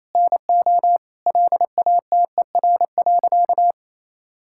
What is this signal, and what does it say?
Morse code, heard as a keyed tone, about 28 wpm, 700 Hz. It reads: NO LATER.
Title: Sentences from Top 1000 Words - Review Part 5 - 28wpm